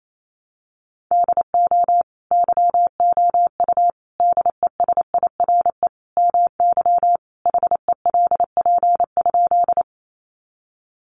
Morse code, heard as a keyed tone, about 28 wpm, 700 Hz. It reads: DO YOU DESIRE MY HELP?
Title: Sentences from Top 1100 Words 28wpm